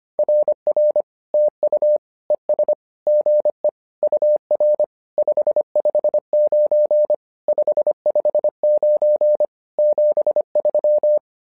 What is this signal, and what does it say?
Morse code, heard as a keyed tone, about 25 wpm, 600 Hz. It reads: RR TU ES GE UR 559 559 73
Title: SOTA QSO Practice 25wpm